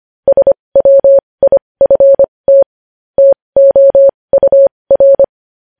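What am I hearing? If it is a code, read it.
Morse code, heard as a keyed tone, about 25 wpm, spaced as for 20 wpm, 550 Hz. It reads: SWIFT TOUR
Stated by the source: News headlines sent in Morse code at 20.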